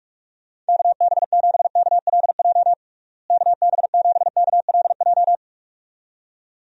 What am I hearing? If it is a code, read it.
Morse code, heard as a keyed tone, about 46 wpm, 700 Hz. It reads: KB7KLJ KB7KLJ
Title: USA Call Signs - 6 Characters - Repeated 2x - 45wpm